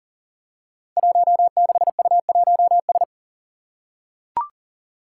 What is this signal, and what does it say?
Morse code, heard as a keyed tone, about 39 wpm, 700 Hz. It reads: 16U1S
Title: Five Challenging Letter-Number Combinations 40wpm